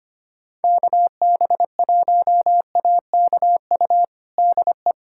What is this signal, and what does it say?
Morse code, heard as a keyed tone, about 25 wpm, 700 Hz. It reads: KB1AKU DE
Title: USA Callsign DE Callsign - 6 Characters - 25wpm